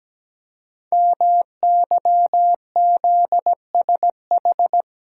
Morse code, heard as a keyed tone, about 17 wpm, 700 Hz. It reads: MYZSH
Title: Code Groups - 5 Letter:Numbers - 17wpm